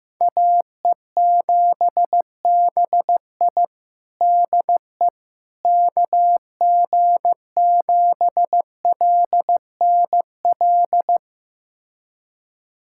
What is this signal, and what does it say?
Morse code, heard as a keyed tone, about 15 wpm, 700 Hz. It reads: AE7BI DE KG7LNL
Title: USA Callsign DE Callsign 15wpm